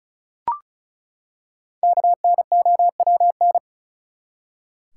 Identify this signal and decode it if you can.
Morse code, heard as a keyed tone, about 34 wpm, 700 Hz. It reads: KNOWN
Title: Five Letter Words - 35wpm